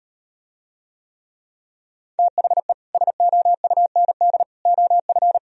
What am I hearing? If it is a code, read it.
Morse code, heard as a keyed tone, about 38 wpm, 700 Hz. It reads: THE SOUND OF